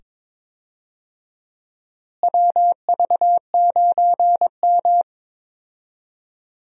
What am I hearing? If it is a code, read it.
Morse code, heard as a keyed tone, about 22 wpm, 700 Hz. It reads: WV9M
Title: USA Call Signs - 4 Characters - Repeated 2x - 22wpm